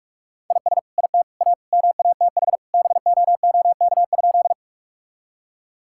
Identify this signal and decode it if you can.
Morse code, heard as a keyed tone, about 47 wpm, 700 Hz. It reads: IS IT A MATH BOOK?
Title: Sentences from Top 1000 Words - Review Part 6 - 45wpm